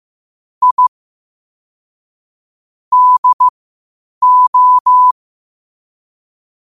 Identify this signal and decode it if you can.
Morse code, heard as a keyed tone, about 15 wpm, spaced as for 8 wpm, 1000 Hz. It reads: I DO
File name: qotdfiles / 8wpm